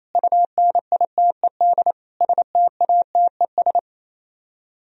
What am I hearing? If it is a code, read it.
Morse code, heard as a keyed tone, about 34 wpm, 700 Hz. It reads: UNITED STATES
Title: Speed-Racing - Sets of 4 Words 28wpm